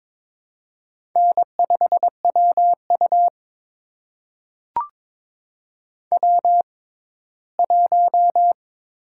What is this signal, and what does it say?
Morse code, heard as a keyed tone, about 22 wpm, spaced as for 15 wpm, 700 Hz. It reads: N5WU W1
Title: USA Call Signs - 6x Character Spacing - 22wpm